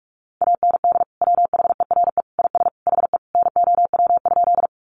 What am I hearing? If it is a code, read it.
Morse code, heard as a keyed tone, about 44 wpm, 700 Hz. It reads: AND WHERE IS HE NOW?
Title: Sentences from Top 200 Words 45wpm